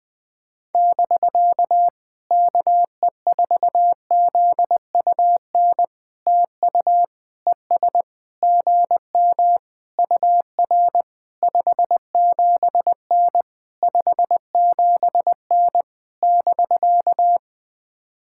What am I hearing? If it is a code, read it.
Morse code, heard as a keyed tone, about 20 wpm, 700 Hz. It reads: <BK> KE4ZUN TU ES GM UR 57N 57N <BK>
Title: POTA QSO Practice - 20wpm